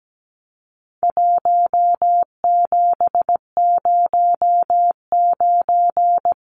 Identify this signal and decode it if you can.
Morse code, heard as a keyed tone, about 17 wpm, 700 Hz. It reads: 1709
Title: Numbers - Contest Style - 4-Digit 17wpm